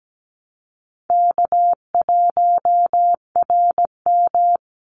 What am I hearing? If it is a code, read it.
Morse code, heard as a keyed tone, about 17 wpm, 700 Hz. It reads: K1RM
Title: Easy Call Signs - 17wpm